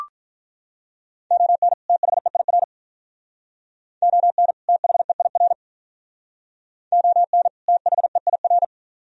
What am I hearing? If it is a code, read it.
Morse code, heard as a keyed tone, about 43 wpm, 700 Hz. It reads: ON THEIR ON THEIR ON THEIR
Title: Speed-Racing - Sets of 2 Words 35wpm